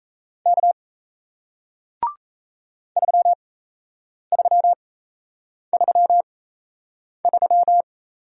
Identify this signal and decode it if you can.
Morse code, heard as a keyed tone, about 37 wpm, spaced as for 8 wpm, 700 Hz. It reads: K 3333